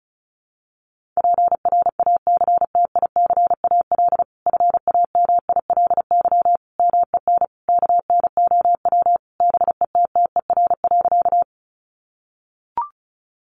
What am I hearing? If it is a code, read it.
Morse code, heard as a keyed tone, about 34 wpm, 700 Hz. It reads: PRACTICAL FAMILY MEN KNOW BETTER.
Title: Sentences from Top 1400 Words 35wpm